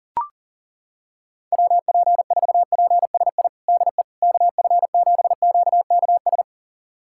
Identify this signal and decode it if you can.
Morse code, heard as a keyed tone, about 40 wpm, 700 Hz. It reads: WP4PSI DE KF7QKS
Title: USA Callsign DE Callsign - 6 Characters - 40wpm